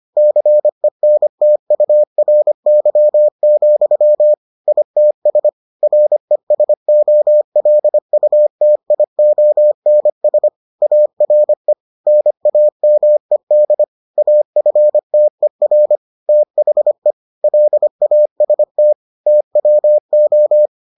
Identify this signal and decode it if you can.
Morse code, heard as a keyed tone, about 25 wpm, 600 Hz. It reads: CENTURY, ITS RESOLUTIONS ARE NAMED AFTER THE LAST TWO